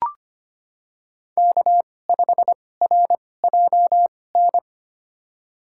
Morse code, heard as a keyed tone, about 25 wpm, spaced as for 18 wpm, 700 Hz. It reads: K5RJN
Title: USA Call Signs - 5 Characters - 2x Character Spacing - 25wpm